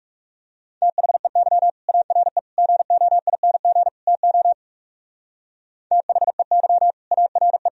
Morse code, heard as a keyed tone, about 38 wpm, 700 Hz. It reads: THEY ARE GOING TO THEY ARE